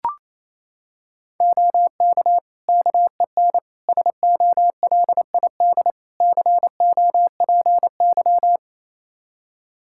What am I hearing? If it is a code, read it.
Morse code, heard as a keyed tone, about 28 wpm, 700 Hz. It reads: OK KEN SOLID COPY
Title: Practice QSO - Exchange 3 - More Info - 28wpm